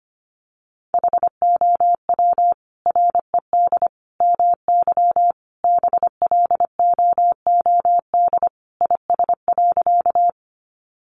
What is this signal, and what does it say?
Morse code, heard as a keyed tone, about 25 wpm, 690 Hz. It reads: HOW RED MY BLOOD IS.
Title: Sentences from Top 1100 Words 25wpm